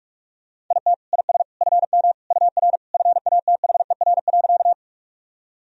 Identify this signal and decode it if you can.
Morse code, heard as a keyed tone, about 41 wpm, 700 Hz. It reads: IT IS FM UR FATHER.
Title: Sentences from Top 300 Words and Abbreviations 45wpm